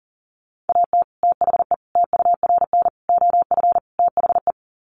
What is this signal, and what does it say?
Morse code, heard as a keyed tone, about 40 wpm, 700 Hz. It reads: AT THE TURN OF THE
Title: Sets of 5 Words 40wpm